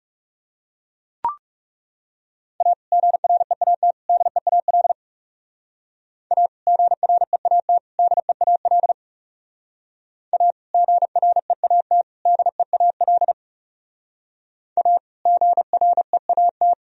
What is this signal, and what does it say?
Morse code, heard as a keyed tone, about 38 wpm, 700 Hz. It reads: A GREAT DEAL A GREAT DEAL A GREAT DEAL A GREAT